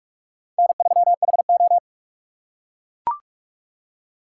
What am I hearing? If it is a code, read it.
Morse code, heard as a keyed tone, about 44 wpm, 700 Hz. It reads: N3HO
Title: USA Call Signs - 4 Characters - 6x Character Spacing - 45wpm